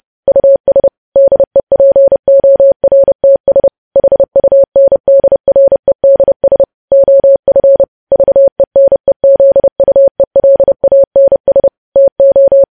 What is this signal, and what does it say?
Morse code, heard as a keyed tone, about 30 wpm, 550 Hz. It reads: US DEPORTS HUNDREDS OF VENEZUELANS TO
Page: News Headlines in Morse Code at 30 WPM